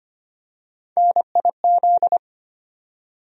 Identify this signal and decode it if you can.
Morse code, heard as a keyed tone, about 25 wpm, 700 Hz. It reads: NIZ
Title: USA Call Sign Postfixes - 3 Letters 25wpm